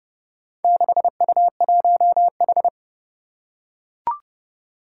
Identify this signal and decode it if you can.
Morse code, heard as a keyed tone, about 28 wpm, 700 Hz. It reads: 6U1H
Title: Four Challenging Letter-Number Combinations 30wpm